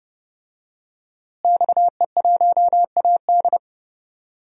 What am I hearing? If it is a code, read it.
Morse code, heard as a keyed tone, about 32 wpm, 700 Hz. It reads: XE1AD
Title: DXCC-100 Call Signs - 5 Characters - Repeated 2x - 30wpm